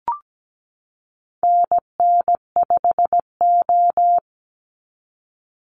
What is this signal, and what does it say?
Morse code, heard as a keyed tone, about 17 wpm, 700 Hz. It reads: NN5O
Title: Call Signs Repeated 3x - 17wpm